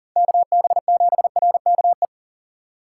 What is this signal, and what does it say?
Morse code, heard as a keyed tone, about 40 wpm, 700 Hz. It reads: KB7RKE
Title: USA Call Signs - 6 Characters - 2x Character Spacing - 40wpm